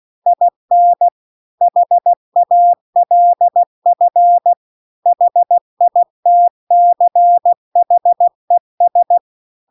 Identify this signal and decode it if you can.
Morse code, heard as a keyed tone, about 16 wpm, 700 Hz. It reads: IN HALF HITCHES